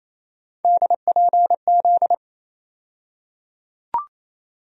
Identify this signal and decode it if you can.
Morse code, heard as a keyed tone, about 28 wpm, 700 Hz. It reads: DPZ
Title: USA Call Sign Postfixes - 3 Letters 28wpm